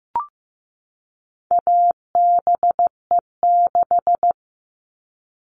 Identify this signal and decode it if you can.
Morse code, heard as a keyed tone, about 15 wpm, 700 Hz. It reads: ABE6
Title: All Cut Numbers - 4-Digit 15wpm